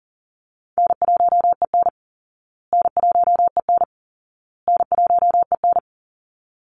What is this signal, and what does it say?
Morse code, heard as a keyed tone, about 40 wpm, 700 Hz. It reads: N1EN N1EN N1EN
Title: Easy Call Signs Repeated 3x - 40wpm